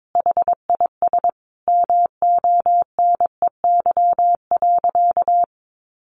Morse code, heard as a keyed tone, about 22 wpm, 700 Hz. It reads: HIS MONEY.
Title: Sentences from Top 1500 Words 22wpm